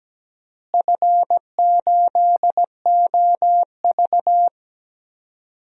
Morse code, heard as a keyed tone, about 17 wpm, 700 Hz. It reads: F8OV
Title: DXCC-100 Call Signs - 4 Characters - Repeated 3x - 17wpm